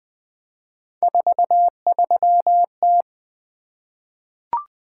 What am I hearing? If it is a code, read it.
Morse code, heard as a keyed tone, about 20 wpm, 700 Hz. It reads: 43T
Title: Three Letter-Number Combinations - 20wpm